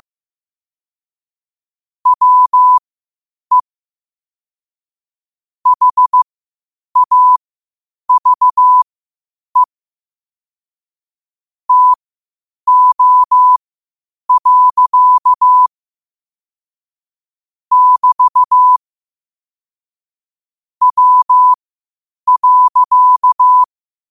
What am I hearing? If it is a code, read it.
Morse code, heard as a keyed tone, about 15 wpm, spaced as for 8 wpm, 1000 Hz. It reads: WE HAVE TO. <BT> W.